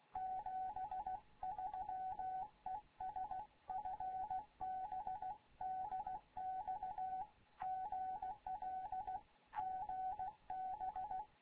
Morse code, heard as a keyed tone, about 16 wpm, 750 Hz. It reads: 73 ES FB DX GL GB
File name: Morze-HA2MN / hangp / 2 11 GYAKORLAT